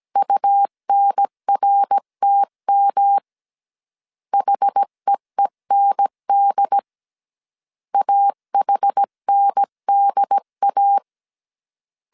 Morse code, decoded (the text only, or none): UNRTM HEEND AHNDA